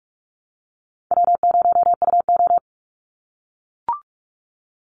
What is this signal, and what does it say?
Morse code, heard as a keyed tone, about 44 wpm, 700 Hz. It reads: W0UO
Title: Call Signs - 45wpm